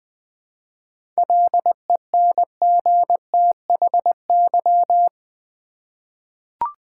Morse code, heard as a keyed tone, about 20 wpm, 700 Hz. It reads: LENGTHY
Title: 7 Letter Words - First 3 Letters Given - 20wpm